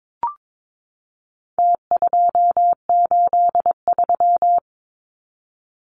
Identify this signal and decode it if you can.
Morse code, heard as a keyed tone, about 22 wpm, 700 Hz. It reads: T283
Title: Numbers - Contest Style - 4-Digit 22wpm